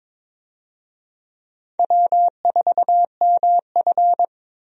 Morse code, heard as a keyed tone, about 22 wpm, 700 Hz. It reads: W4MF
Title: USA Call Signs - 4 Characters - 22wpm